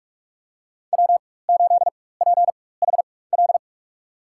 Morse code, heard as a keyed tone, about 41 wpm, spaced as for 20 wpm, 700 Hz. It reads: W8PHL
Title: USA Call Signs - 5 Characters - 4x Character Spacing - 45wpm